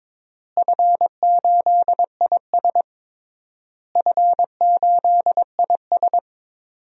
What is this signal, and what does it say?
Morse code, heard as a keyed tone, about 22 wpm, 700 Hz. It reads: F8IS F8IS
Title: DXCC-100 Call Signs - 4 Characters - Repeated 3x - 22wpm